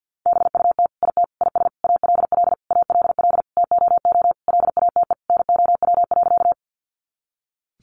Morse code, heard as a keyed tone, about 50 wpm, 700 Hz. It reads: BUT IT IS ALL ALL TOO LATE NOW.
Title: Sentences from Top 300 Words 50wpm